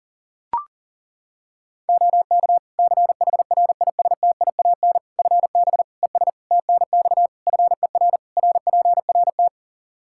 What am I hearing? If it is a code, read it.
Morse code, heard as a keyed tone, about 40 wpm, 700 Hz. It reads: OK CHRISTIAN FB ES TNX FER RPRT
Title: Practice QSO - Exchange 2 - Info - 40wpm